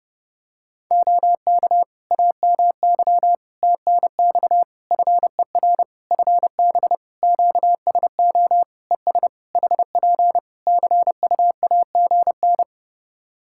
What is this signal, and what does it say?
Morse code, heard as a keyed tone, about 31 wpm, 700 Hz. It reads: OK AMY TNX FER FB QSO ES HP CUAGN
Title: Practice QSO - Exchange 4 - Ending - 30wpm